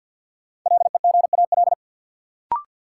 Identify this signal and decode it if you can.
Morse code, heard as a keyed tone, about 53 wpm, 700 Hz. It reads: LEGAL
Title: Sentences from Top 1200 Words - New Word Drill - 50wpm